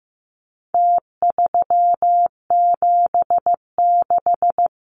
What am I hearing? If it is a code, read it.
Morse code, heard as a keyed tone, about 15 wpm, 700 Hz. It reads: T376
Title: Numbers - Contest Style - 4-Digit 15wpm